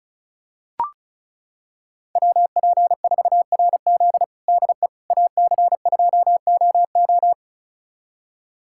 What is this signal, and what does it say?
Morse code, heard as a keyed tone, about 36 wpm, 700 Hz. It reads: WP4RZ DE AC2OO
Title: USA Callsign DE Callsign - 5 Characters - 35wpm